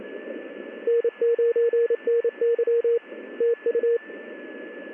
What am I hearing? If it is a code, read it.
Morse code, heard as a keyed tone, about 28 wpm, 470 Hz, with radio noise. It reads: N9NY TU
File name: contests / ss-ssb-k5tr- -1700